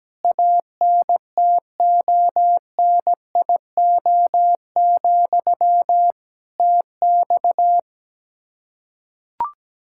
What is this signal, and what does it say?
Morse code, heard as a keyed tone, about 17 wpm, 700 Hz. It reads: ANTONIO, TX
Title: Practice QSO - Exchange 1 - Beginning - 17wpm